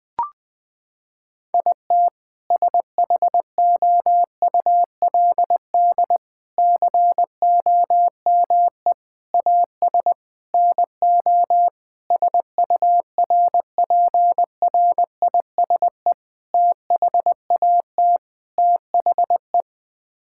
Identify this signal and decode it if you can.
Morse code, heard as a keyed tone, about 20 wpm, 700 Hz. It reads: IT SHOULD COME AS NO SURPRISE THAT THE
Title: Sets of 8 Words - 20wpm